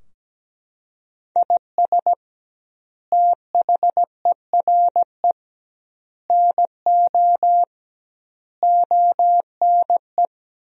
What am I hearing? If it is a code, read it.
Morse code, heard as a keyed tone, about 17 wpm, 700 Hz. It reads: IS THERE NO ONE
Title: 2x Word-Spacing - Sentences from Top 200 Words - 17wpm